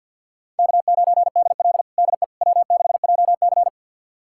Morse code, heard as a keyed tone, about 50 wpm, 700 Hz. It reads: K9DL DE W6JC
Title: USA Callsign DE Callsign - 4 Characters - 50wpm